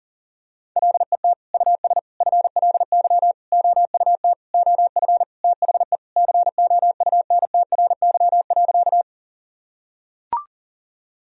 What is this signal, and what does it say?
Morse code, heard as a keyed tone, about 40 wpm, 700 Hz. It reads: LET US FLY OUT OF THE COUNTRY.
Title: Sentences from Top 1000 Words - Review Part 2 - 40wpm